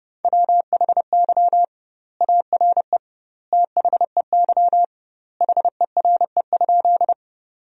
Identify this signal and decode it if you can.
Morse code, heard as a keyed tone, about 32 wpm, 700 Hz. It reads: WHY ARE THEY HERE?
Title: 2x Word-Spacing - Sentences from Top 200 Words - 30wpm